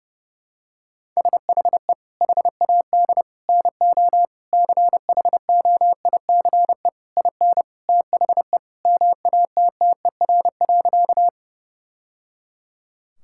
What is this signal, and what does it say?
Morse code, heard as a keyed tone, about 32 wpm, 700 Hz. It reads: SHE HAD NO CHOICE IN THE MATTER.
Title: Sentences from Top 1000 Words - Review Part 6 - 30wpm